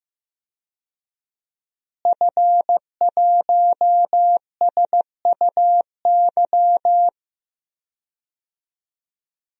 Morse code, heard as a keyed tone, about 15 wpm, 700 Hz. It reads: F1SUY
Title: DXCC-100 Call Signs - 5 Characters - Repeated 2x - 15wpm